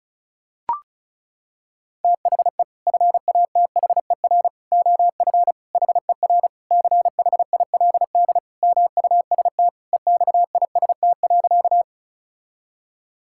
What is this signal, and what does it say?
Morse code, heard as a keyed tone, about 34 wpm, 700 Hz. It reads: THE FATHER OF HER CHILD MUST EXIST.
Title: Sentences from Top 1000 Words - Review Part 5 - 35wpm